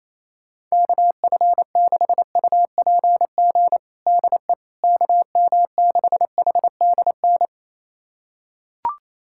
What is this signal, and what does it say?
Morse code, heard as a keyed tone, about 28 wpm, 700 Hz. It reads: KF6UPG DE KM6HDN
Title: USA Callsign DE Callsign - 6 Characters - 28wpm